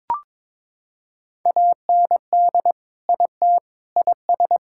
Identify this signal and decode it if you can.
Morse code, heard as a keyed tone, about 21 wpm, 700 Hz. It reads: AND IT IS
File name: practice / 22wpm / Sentences with 3-Word Phrases - 22wpm